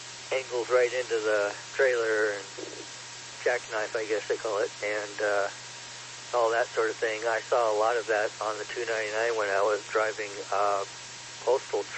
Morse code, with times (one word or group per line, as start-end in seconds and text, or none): none